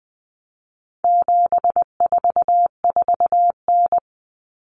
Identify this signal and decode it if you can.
Morse code, heard as a keyed tone, about 20 wpm, 700 Hz. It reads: 744N
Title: Common Cut Numbers - 4-Digit 20wpm